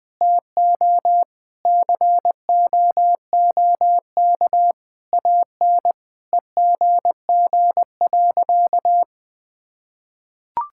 Morse code, heard as a keyed tone, about 20 wpm, 700 Hz. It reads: TO COOK AN EGG.
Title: Sentences with 7-Word Phrases - 20wpm